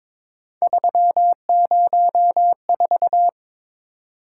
Morse code, heard as a keyed tone, about 26 wpm, 700 Hz. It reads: 304
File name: Speed-Racing - Three Letter-Number Combinations - 22wpm